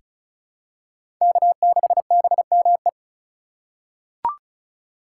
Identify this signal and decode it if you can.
Morse code, heard as a keyed tone, about 36 wpm, 700 Hz. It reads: K6BME
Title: USA Call Signs - 5 Characters - 4x Character Spacing - 35wpm